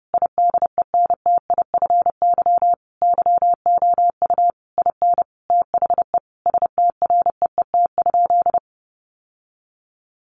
Morse code, heard as a keyed tone, about 30 wpm, 700 Hz. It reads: IDENTIFY YOU IN THE STREET?
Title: Sentences from Top 900 Words 30wpm